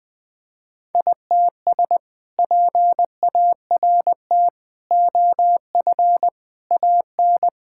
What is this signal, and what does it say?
Morse code, heard as a keyed tone, about 21 wpm, 700 Hz. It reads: ITS PART OF AN
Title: Sentences with 2-Word Phrases - Spoken Beginning - 20wpm